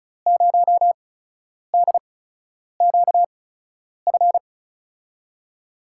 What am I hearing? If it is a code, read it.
Morse code, heard as a keyed tone, about 36 wpm, spaced as for 10 wpm, 700 Hz. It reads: 0DQF
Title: USA Call Signs - 6 Characters - 8x Character Spacing - 35wpm